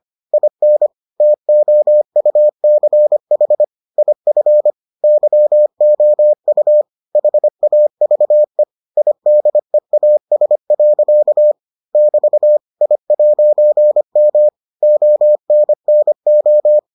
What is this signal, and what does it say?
Morse code, heard as a keyed tone, about 25 wpm, 600 Hz. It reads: IN TOUCH IF YOU HAVE IDEAS. <BT> I'M ONNO